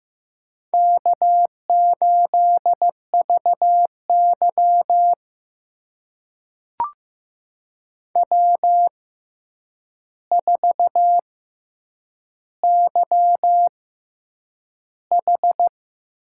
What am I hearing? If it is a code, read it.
Morse code, heard as a keyed tone, about 15 wpm, spaced as for 8 wpm, 700 Hz. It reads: K8VY W4YH
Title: USA Call Signs - 6x Character Spacing - 15wpm